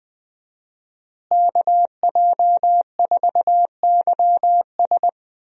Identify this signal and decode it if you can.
Morse code, heard as a keyed tone, about 20 wpm, 700 Hz. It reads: KJ4YS
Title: USA Call Signs - 5 Characters - Repeated 3x - 20wpm